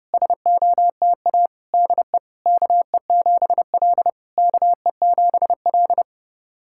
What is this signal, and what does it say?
Morse code, heard as a keyed tone, about 28 wpm, 690 Hz. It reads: SOTA DE KE7L KE7L
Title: SOTA QSO Practice 30wpm